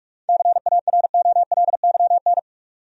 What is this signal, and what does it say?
Morse code, heard as a keyed tone, about 46 wpm, 700 Hz. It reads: CAROLYN
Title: USA Common Names - 45wpm